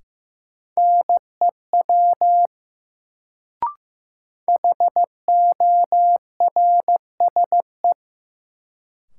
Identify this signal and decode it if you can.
Morse code, heard as a keyed tone, about 15 wpm, 700 Hz. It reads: NEW HORSE
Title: Sentences from Top 300 Words - New Word Drill - 15wpm